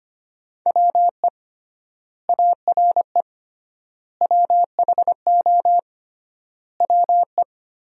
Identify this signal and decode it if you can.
Morse code, heard as a keyed tone, about 25 wpm, 700 Hz. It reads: WE ARE WHO WE
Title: 3x Word-Spacing - Sentences from Top 200 Words - 25wpm